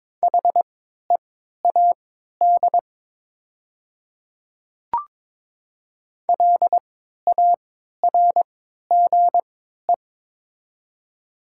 Morse code, heard as a keyed tone, about 22 wpm, spaced as for 12 wpm, 700 Hz. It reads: HEAD LARGE
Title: 3x Character Spacing - Top 300 Words - 22wpm